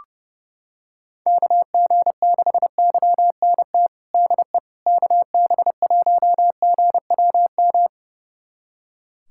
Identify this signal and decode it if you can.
Morse code, heard as a keyed tone, about 29 wpm, 700 Hz. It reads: KG6YNT DE KB1GWM
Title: USA Callsign DE Callsign - 6 Characters - 30wpm